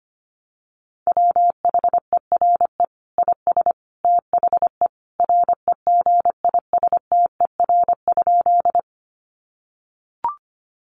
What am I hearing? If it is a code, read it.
Morse code, heard as a keyed tone, about 25 wpm, 700 Hz. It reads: WHERE IS THE REGISTER?
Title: Sentences from Top 1200 Words 25wpm